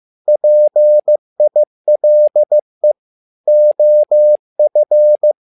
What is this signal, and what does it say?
Morse code, heard as a keyed tone, about 15 wpm, 600 Hz. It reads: PILE OF
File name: learning center / tower trouble / tower trouble 15wpm